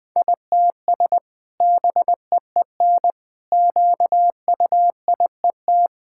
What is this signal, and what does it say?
Morse code, heard as a keyed tone, about 20 wpm, 700 Hz. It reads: ITS BEEN QUIET
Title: Sentences with 2-Word Phrases - 20wpm